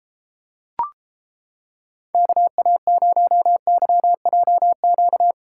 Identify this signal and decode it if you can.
Morse code, heard as a keyed tone, about 31 wpm, 700 Hz. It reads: KA0YJQ